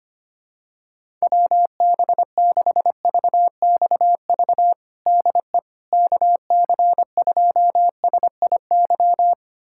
Morse code, heard as a keyed tone, about 25 wpm, 700 Hz. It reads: WB6VXV DE KC2SIY
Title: USA Callsign DE Callsign 25wpm